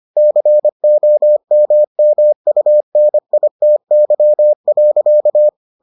Morse code, heard as a keyed tone, about 25 wpm, 600 Hz. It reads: COMMUNITY.